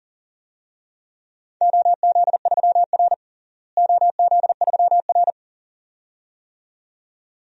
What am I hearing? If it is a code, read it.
Morse code, heard as a keyed tone, about 40 wpm, 700 Hz. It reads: OZ3R OZ3R
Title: DXCC-100 Call Signs - 4 Characters - Repeated 2x - 40wpm